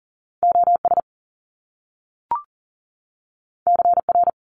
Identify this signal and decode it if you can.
Morse code, heard as a keyed tone, about 40 wpm, 700 Hz. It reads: OS CR
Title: Top Two Letter Combinations in Words 40wpm